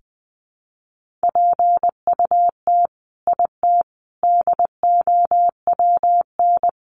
Morse code, heard as a keyed tone, about 20 wpm, 700 Hz. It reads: PUT IT DOWN
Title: Sentences from Top 1400 Words 20wpm